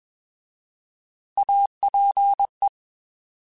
Morse code, heard as a keyed tone, about 21 wpm, 800 Hz. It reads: APE
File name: Alphabetical / Alphabetical mp3s / As @20